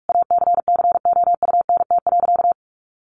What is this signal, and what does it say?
Morse code, heard as a keyed tone, about 44 wpm, 700 Hz. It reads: ACCOUNT.
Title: Sentences from Top 1300 Words 45wpm